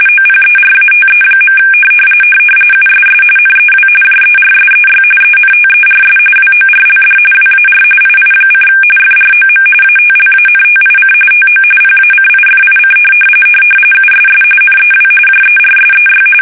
RTTY, about 45.45 baud. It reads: OPERATE. USS NEW JERSEY ENTERS WORLD WAR II: FROM ATLANTIC TO PACIFIC USS NEW JERSEY
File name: fsk850 / battleshipnj